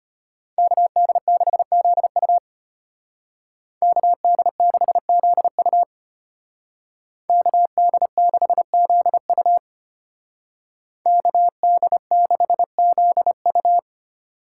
Morse code, heard as a keyed tone, about 32 wpm, 700 Hz. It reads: KD6ZU KD6ZU KD6ZU KD6ZU